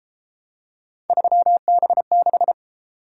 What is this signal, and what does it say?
Morse code, heard as a keyed tone, about 31 wpm, 700 Hz. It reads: 3B6